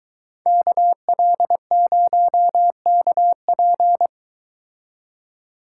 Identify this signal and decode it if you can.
Morse code, heard as a keyed tone, about 22 wpm, 700 Hz. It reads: KL0KP